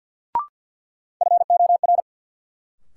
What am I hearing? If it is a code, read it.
Morse code, heard as a keyed tone, about 54 wpm, 700 Hz. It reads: FOR
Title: Sentences from Top 100 Words - New Word Drill - 50wpm